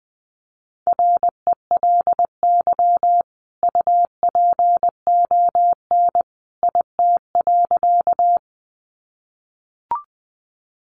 Morse code, heard as a keyed tone, about 20 wpm, 700 Hz. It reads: RELY UPON IT.
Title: Sentences from Top 1200 Words 20wpm